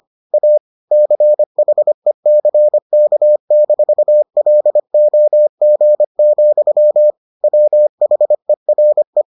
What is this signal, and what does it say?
Morse code, heard as a keyed tone, about 25 wpm, 600 Hz. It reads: A CHECK-LOG, WHERE